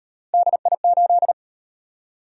Morse code, heard as a keyed tone, about 38 wpm, 700 Hz. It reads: DI8